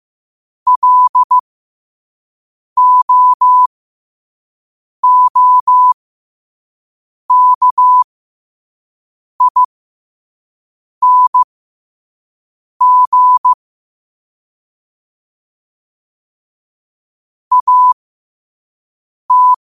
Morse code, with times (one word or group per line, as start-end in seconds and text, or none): 0.7-13.5 LOOKING
17.5-19.5 AT